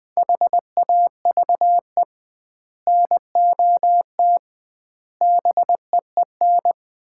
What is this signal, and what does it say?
Morse code, heard as a keyed tone, about 20 wpm, 700 Hz. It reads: HAVE NOT BEEN
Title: 2x Word-Spacing - Sets of 5 Words - 20wpm